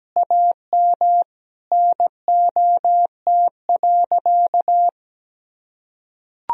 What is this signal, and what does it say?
Morse code, heard as a keyed tone, about 17 wpm, 700 Hz. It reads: AM NOT.
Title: Sentences from Top 1000 Words - Review Part 2 - 17wpm